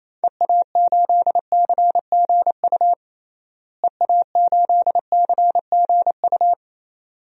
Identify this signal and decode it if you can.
Morse code, heard as a keyed tone, about 28 wpm, 700 Hz. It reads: EA8CGU EA8CGU
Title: DXCC-100 Call Signs - 6 Characters - Repeated 2x - 28wpm